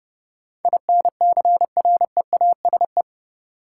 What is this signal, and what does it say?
Morse code, heard as a keyed tone, about 30 wpm, 700 Hz. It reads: INCREASE
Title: Sets of 5 Words - New Word Drill - 30wpm